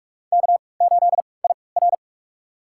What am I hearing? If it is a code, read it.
Morse code, heard as a keyed tone, about 44 wpm, spaced as for 25 wpm, 700 Hz. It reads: K8IR
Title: USA Call Signs - 4 Characters - 3x Character Spacing - 45wpm